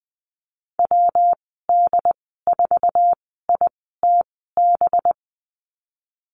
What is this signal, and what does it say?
Morse code, heard as a keyed tone, about 19 wpm, spaced as for 14 wpm, 700 Hz. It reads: WD4ITB
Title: 207 Lesson 33g - Letter j - Call Signs 20wpm